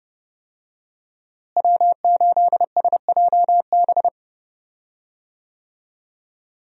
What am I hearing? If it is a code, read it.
Morse code, heard as a keyed tone, about 32 wpm, 700 Hz. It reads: W8SJB
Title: USA Call Signs - 5 Characters - Repeated 3x - 30wpm